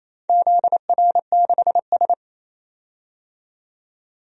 Morse code, heard as a keyed tone, about 28 wpm, 700 Hz. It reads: ZR6S
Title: DXCC-100 Call Signs - 4 Characters - 28wpm